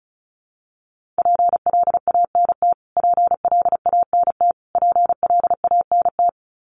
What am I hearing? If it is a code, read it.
Morse code, heard as a keyed tone, about 34 wpm, 700 Hz. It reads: PLANT PLANT PLANT
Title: Top 300 Words Repeated 35wpm